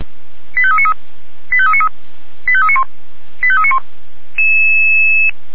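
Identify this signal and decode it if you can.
Five-tone selective call (ZVEI), heard as quick runs of five tones, tones of about 70 ms. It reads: 85393 85393 85391 85391
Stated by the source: Five-tone ZVEI selective call 89414, received on SDR.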